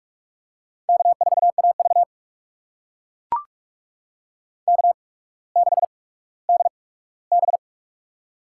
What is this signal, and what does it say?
Morse code, heard as a keyed tone, about 44 wpm, spaced as for 19 wpm, 700 Hz. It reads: K4AV K6DB
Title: USA Call Signs - 4 Characters - 8x Character Spacing - 45wpm